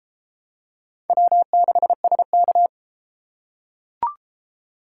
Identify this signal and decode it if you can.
Morse code, heard as a keyed tone, about 30 wpm, 700 Hz. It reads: W6SK